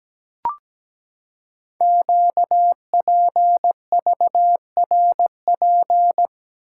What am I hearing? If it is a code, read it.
Morse code, heard as a keyed tone, about 17 wpm, 700 Hz. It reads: QPVRP
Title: Code Groups - 5 Letters - 17wpm